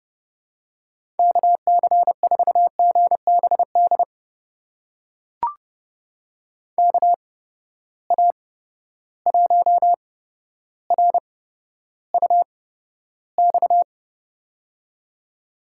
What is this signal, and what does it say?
Morse code, heard as a keyed tone, about 32 wpm, spaced as for 13 wpm, 700 Hz. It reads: KC4GBD KA1RUX
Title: USA Call Signs - 6 Characters - 8x Character Spacing - 30wpm